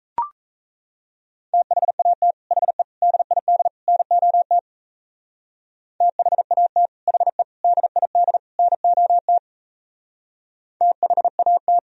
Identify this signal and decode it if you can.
Morse code, heard as a keyed tone, about 33 wpm, 700 Hz. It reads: THAT HE DID NOT THAT HE DID NOT THAT
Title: Speed-Racing - Sets of 4 Words 28wpm